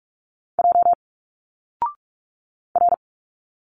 Morse code, heard as a keyed tone, about 46 wpm, 700 Hz. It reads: J R
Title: Single Letters 45wpm